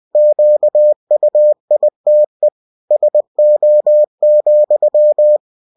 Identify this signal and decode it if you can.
Morse code, heard as a keyed tone, about 20 wpm, 600 Hz. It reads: QUITE SO,
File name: cw-books / winnie-the-pooh / winnie-20wpm-page-0084